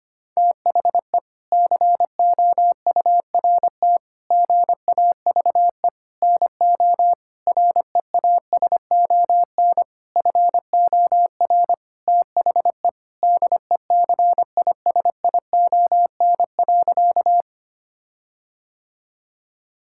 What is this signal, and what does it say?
Morse code, heard as a keyed tone, about 25 wpm, 700 Hz. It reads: THE COURT GAVE NO REASON FOR THE DECISION.
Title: Sentences from Top 1600 Words 25wpm